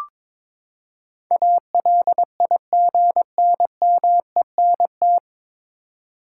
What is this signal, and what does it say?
Morse code, heard as a keyed tone, about 21 wpm, 700 Hz. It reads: ALIGNMENT
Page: Morse Code Ninja practice 22wpm